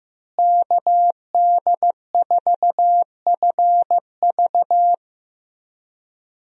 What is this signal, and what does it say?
Morse code, heard as a keyed tone, about 15 wpm, 700 Hz. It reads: KD4FV
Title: USA Call Signs - 5 Characters - Repeated 2x - 15wpm